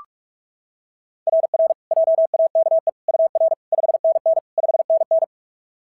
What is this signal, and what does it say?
Morse code, heard as a keyed tone, about 44 wpm, 690 Hz. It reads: RR JAKE UR 5NN 5NN
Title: SOTA QSO Practice 45wpm